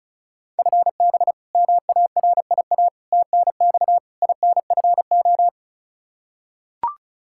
Morse code, heard as a keyed tone, about 35 wpm, 700 Hz. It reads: FB MARIA TNX INFO
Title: K1USN SST QSO Practice 35wpm